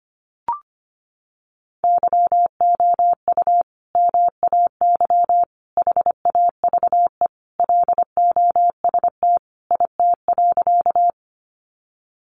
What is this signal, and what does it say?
Morse code, heard as a keyed tone, about 25 wpm, 700 Hz. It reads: YOU MAY HAVE LOST IT.
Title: Sentences from Top 1400 Words 25wpm